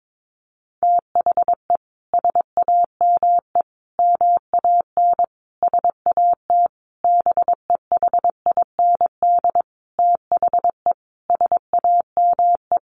Morse code, heard as a keyed tone, about 22 wpm, 700 Hz. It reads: THE SAME MAN SAT BEHIND THE SAME
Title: A Sound Of Thunder 22wpm